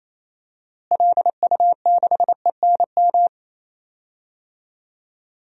Morse code, heard as a keyed tone, about 28 wpm, 700 Hz. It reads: LU6ENM
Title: DXCC-100 Call Signs - 6 Characters - Repeated 3x - 28wpm